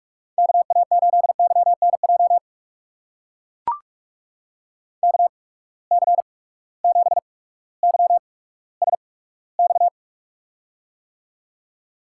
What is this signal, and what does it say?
Morse code, heard as a keyed tone, about 45 wpm, spaced as for 19 wpm, 700 Hz. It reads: KA8YNJ KC7YSX
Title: USA Call Signs - 6 Characters - 8x Character Spacing - 45wpm